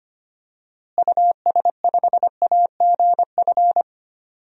Morse code, heard as a keyed tone, about 25 wpm, 700 Hz. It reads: US5AGF
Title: DXCC-100 Call Signs - 6 Characters - Repeated 2x - 25wpm